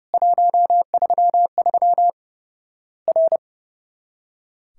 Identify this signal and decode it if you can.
Morse code, heard as a keyed tone, about 30 wpm, 700 Hz. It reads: 133 R
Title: Worked All Europe DX Contest - QTC Exchange - 30wpm